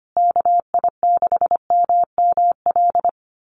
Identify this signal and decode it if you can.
Morse code, heard as a keyed tone, about 25 wpm, 700 Hz. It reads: KI6MML
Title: USA Call Signs Repeated 3x 25wpm